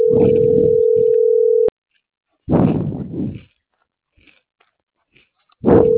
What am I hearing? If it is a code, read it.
Ringback tone (North America), 440 + 480 Hz.